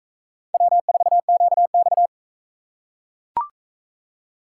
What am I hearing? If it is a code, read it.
Morse code, heard as a keyed tone, about 44 wpm, 700 Hz. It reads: W4QX